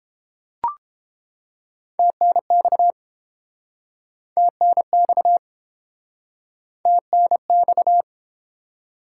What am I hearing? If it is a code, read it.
Morse code, heard as a keyed tone, about 30 wpm, 700 Hz. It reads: TNX TNX TNX